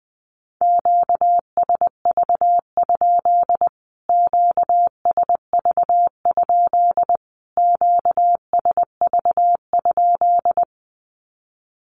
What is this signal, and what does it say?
Morse code, heard as a keyed tone, about 20 wpm, 700 Hz. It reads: QSV? QSV? QSV?
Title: Q-Codes Repeated 3x 20wpm